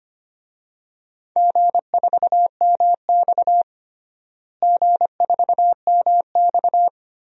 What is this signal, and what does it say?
Morse code, heard as a keyed tone, about 25 wpm, 700 Hz. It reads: G4MX G4MX
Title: DXCC-100 Call Signs - 4 Characters - Repeated 3x - 25wpm